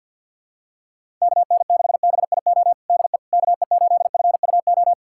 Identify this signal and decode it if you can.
Morse code, heard as a keyed tone, about 48 wpm, 700 Hz. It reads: KN6BIO DE KE8FUO
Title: USA Callsign DE Callsign - 6 Characters - 50wpm